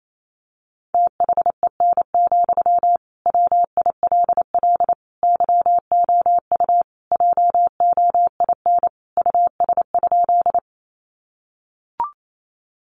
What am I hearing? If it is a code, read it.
Morse code, heard as a keyed tone, about 28 wpm, 700 Hz. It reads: THEN, WILL YOU JOIN US?
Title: Sentences from Top 1400 Words 28wpm